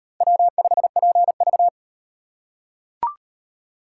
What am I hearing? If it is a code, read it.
Morse code, heard as a keyed tone, about 37 wpm, 700 Hz. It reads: W5PV